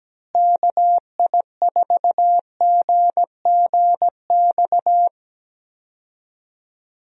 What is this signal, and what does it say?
Morse code, heard as a keyed tone, about 17 wpm, 700 Hz. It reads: KI4GGX
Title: USA Call Signs - 6 Characters - Repeated 3x - 17wpm